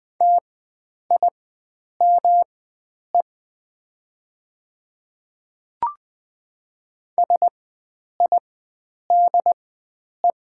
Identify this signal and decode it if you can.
Morse code, heard as a keyed tone, about 20 wpm, spaced as for 9 wpm, 690 Hz. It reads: TIME SIDE
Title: 4x Character Spacing - Top 300 Words - 20wpm